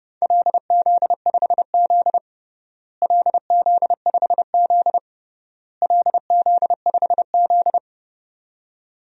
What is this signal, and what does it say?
Morse code, heard as a keyed tone, about 30 wpm, 700 Hz. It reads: LZ5Z LZ5Z LZ5Z
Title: DXCC-100 Call Signs - 4 Characters - Repeated 3x - 30wpm